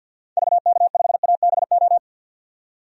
Morse code, heard as a keyed tone, about 50 wpm, 700 Hz. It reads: VK5ABO
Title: DXCC-100 Call Signs - 6 Characters - 50wpm